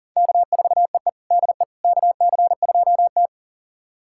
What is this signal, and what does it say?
Morse code, heard as a keyed tone, about 40 wpm, 690 Hz. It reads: K4EE DE KC2T
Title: USA Callsign DE Callsign - 4 Characters - 40wpm